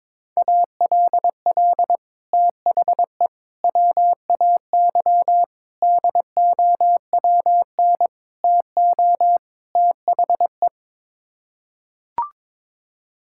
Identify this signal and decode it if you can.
Morse code, heard as a keyed tone, about 22 wpm, 700 Hz. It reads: ALL THE WAY DOWN TO THE
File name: practice / 22wpm / Sets of 6 Words - 22wpm